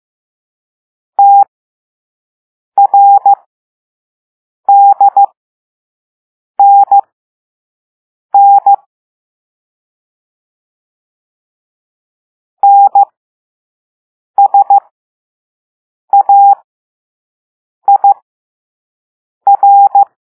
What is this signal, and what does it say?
Morse code, heard as a keyed tone, about 15 wpm, spaced as for 5 wpm, 800 Hz. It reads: TRDNN NSAIR